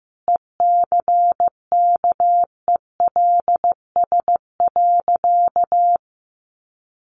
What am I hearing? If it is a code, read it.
Morse code, heard as a keyed tone, about 15 wpm, 700 Hz. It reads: ECKELS.
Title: A Sound Of Thunder 15wpm